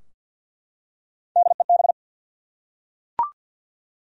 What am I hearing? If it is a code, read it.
Morse code, heard as a keyed tone, about 52 wpm, 700 Hz. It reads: DEB
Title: USA Common Names - Max 5 Letters - 50wpm